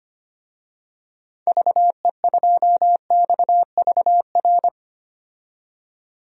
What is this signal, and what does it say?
Morse code, heard as a keyed tone, about 25 wpm, 700 Hz. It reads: VE2XVR
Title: DXCC-100 Call Signs - 6 Characters - 25wpm